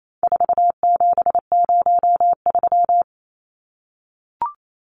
Morse code, heard as a keyed tone, about 28 wpm, 700 Hz. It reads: 4703
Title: Numbers - 4-Digit 28wpm